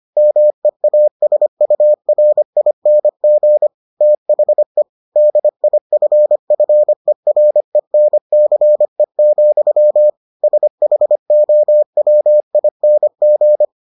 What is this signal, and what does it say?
Morse code, heard as a keyed tone, about 25 wpm, 600 Hz. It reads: MEASURING THE DIFFERENCE, SHOWING